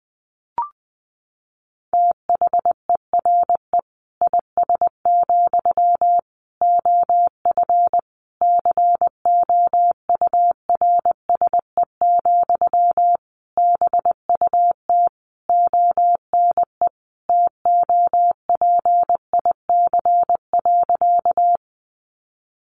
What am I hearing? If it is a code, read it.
Morse code, heard as a keyed tone, about 20 wpm, 700 Hz. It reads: THERE IS, OF COURSE, BUT ONE TOPIC.
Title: Sentences from Top 1200 Words 20wpm